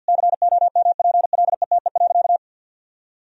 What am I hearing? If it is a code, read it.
Morse code, heard as a keyed tone, about 46 wpm, 700 Hz. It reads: COMPLETE.
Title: Sentences from Top 400 Words and Abbreviations 50wpm